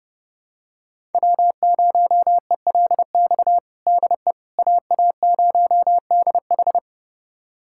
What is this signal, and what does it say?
Morse code, heard as a keyed tone, about 30 wpm, 700 Hz. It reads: W0ELX DE AA0DH
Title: USA Callsign DE Callsign - 5 Characters - 30wpm